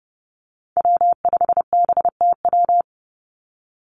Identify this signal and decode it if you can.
Morse code, heard as a keyed tone, about 30 wpm, 700 Hz. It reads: W5BTW
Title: USA Call Signs 30wpm